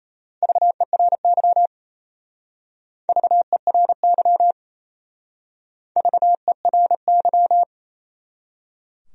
Morse code, heard as a keyed tone, about 32 wpm, 700 Hz. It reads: VERY VERY VERY
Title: Speed-Racing Top 200 Words 28wpm